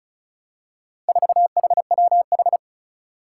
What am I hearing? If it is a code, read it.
Morse code, heard as a keyed tone, about 36 wpm, 700 Hz. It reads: 4HWH
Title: Four Challenging Letter-Number Combinations 35wpm